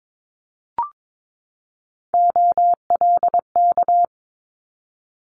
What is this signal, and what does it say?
Morse code, heard as a keyed tone, about 22 wpm, 700 Hz. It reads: OLK
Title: Multi-letter Phonemes 22wpm